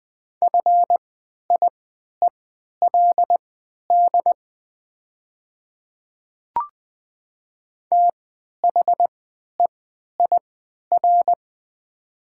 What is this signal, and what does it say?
Morse code, heard as a keyed tone, about 21 wpm, spaced as for 11 wpm, 700 Hz. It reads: FIELD THEIR
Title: 3x Character Spacing - Top 500 Words - 20wpm